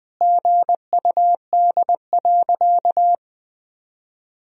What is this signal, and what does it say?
Morse code, heard as a keyed tone, about 20 wpm, 700 Hz. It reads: GUD.
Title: Sentences from Top 200 Words and Abbreviations 20wpm